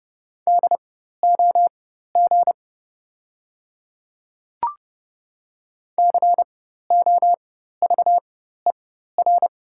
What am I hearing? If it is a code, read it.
Morse code, heard as a keyed tone, about 30 wpm, spaced as for 14 wpm, 700 Hz. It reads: DOG COVER
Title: 4x Character Spacing - Top 500 Words - 30wpm